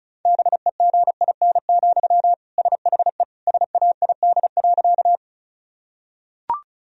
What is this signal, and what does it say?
Morse code, heard as a keyed tone, about 34 wpm, 700 Hz. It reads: BEGIN, SHE SAID.
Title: Sentences from Top 1000 Words - Review Part 1 - 35wpm